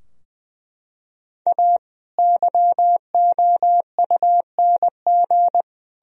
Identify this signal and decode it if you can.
Morse code, heard as a keyed tone, about 20 wpm, 700 Hz. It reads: A YOUNG
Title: Sentences from Top 1000 Words - Review Part 3 - 20wpm